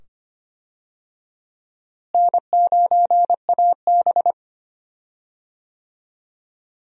Morse code, heard as a keyed tone, about 25 wpm, 700 Hz. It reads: N9AB
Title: USA Call Signs - 4 Characters - Repeated 2x - 25wpm